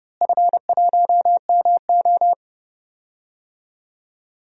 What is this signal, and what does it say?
Morse code, heard as a keyed tone, about 30 wpm, 700 Hz. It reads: F1MO
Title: DXCC-100 Call Signs - 4 Characters - Repeated 3x - 30wpm